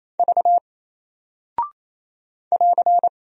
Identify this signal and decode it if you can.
Morse code, heard as a keyed tone, about 28 wpm, 700 Hz. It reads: V <AR>
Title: Single Character (Letter:Number:Punctuation:Prosign) Alt 28wpm